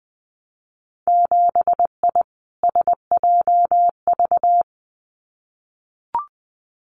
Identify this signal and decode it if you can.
Morse code, heard as a keyed tone, about 20 wpm, 700 Hz. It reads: 7I SJV
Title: Winter Field Day Exchanges Encore 20wpm